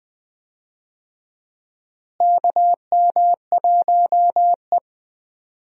Morse code, heard as a keyed tone, about 20 wpm, 700 Hz. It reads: KM1E
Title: USA Call Signs - 4 Characters - Repeated 2x - 20wpm